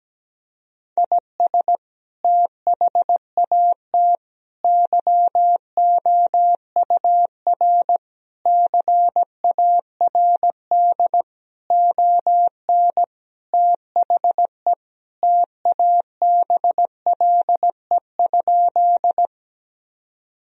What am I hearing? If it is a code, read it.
Morse code, heard as a keyed tone, about 17 wpm, 700 Hz. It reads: IS THAT YOUR CARD ON THE TABLE?
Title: Sentences from Top 1000 Words - Review Part 4 - 17wpm